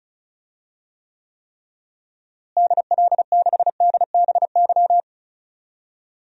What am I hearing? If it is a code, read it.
Morse code, heard as a keyed tone, about 34 wpm, 700 Hz. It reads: DL6DBY
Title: DXCC-100 Call Signs - 6 Characters - Repeated 3x - 35wpm